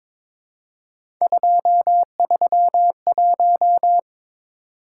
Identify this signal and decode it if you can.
Morse code, heard as a keyed tone, about 21 wpm, 700 Hz. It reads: 231
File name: practice / 22wpm / Three Letter-Number Combinations - 22wpm